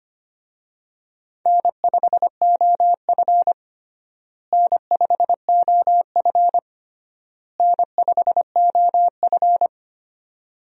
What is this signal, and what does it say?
Morse code, heard as a keyed tone, about 25 wpm, 700 Hz. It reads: N5OF N5OF N5OF
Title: USA Call Signs - 4 Characters - Repeated 3x - 25wpm